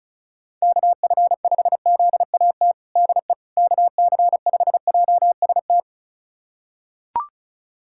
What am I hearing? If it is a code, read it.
Morse code, heard as a keyed tone, about 34 wpm, 700 Hz. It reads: KF5ZAT DE KC5JST
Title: USA Callsign DE Callsign - 6 Characters - 35wpm